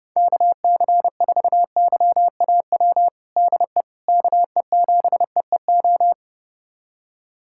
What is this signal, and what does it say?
Morse code, heard as a keyed tone, about 30 wpm, 700 Hz. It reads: KC4YAW DE KE7EEO
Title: USA Callsign DE Callsign 30wpm